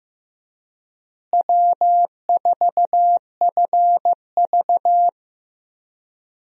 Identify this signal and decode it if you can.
Morse code, heard as a keyed tone, about 15 wpm, 700 Hz. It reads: W4FV
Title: USA Call Signs - 4 Characters - Repeated 2x - 15wpm